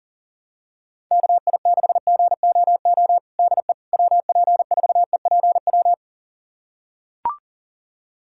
Morse code, heard as a keyed tone, about 37 wpm, 700 Hz. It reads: KI6GOO DE WP4EPW
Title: USA Callsign DE Callsign - 6 Characters - 40wpm